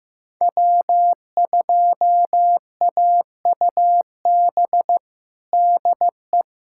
Morse code, heard as a keyed tone, about 15 wpm, 700 Hz. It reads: W2AUB DE
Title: USA Callsign DE Callsign 15wpm